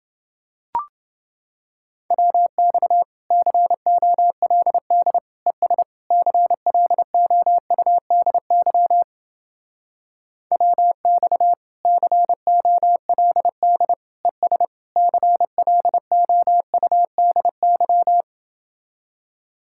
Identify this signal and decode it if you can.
Morse code, heard as a keyed tone, about 30 wpm, 700 Hz. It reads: WX COLD ES CLOUDY WX COLD ES CLOUDY